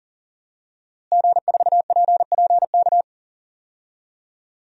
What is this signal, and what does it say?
Morse code, heard as a keyed tone, about 38 wpm, 700 Hz. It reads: G4PPK
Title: DXCC-100 Call Signs - 5 Characters - Repeated 2x - 40wpm